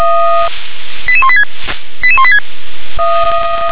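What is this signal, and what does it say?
Five-tone selective call (ZVEI), heard as quick runs of five tones, tones of about 70 ms. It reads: 88186 88186
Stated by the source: ZVEI five-tone selective call received on an SDR.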